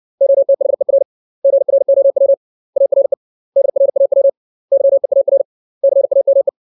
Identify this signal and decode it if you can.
Morse code, heard as a keyed tone, about 60 wpm, 550 Hz. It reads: OTHER GROW ARE DRAW YEAR CAME
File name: 632 words / 632 Words 60WPM